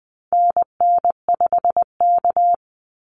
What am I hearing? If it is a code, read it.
Morse code, heard as a keyed tone, about 20 wpm, 700 Hz. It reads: NN5K
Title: NA SOTA Activator Callsigns 20wpm